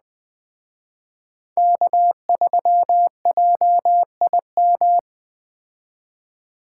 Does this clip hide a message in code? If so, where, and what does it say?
Yes: Morse code from 1.6 to 5.0 s, heard as a keyed tone, about 21 wpm, 700 Hz. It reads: K3JIM